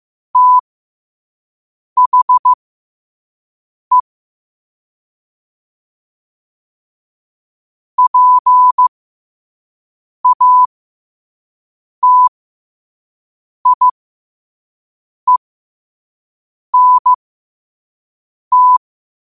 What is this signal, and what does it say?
Morse code, heard as a keyed tone, about 15 wpm, spaced as for 5 wpm, 1000 Hz. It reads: THE PATIENT
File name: qotdfiles / 5wpm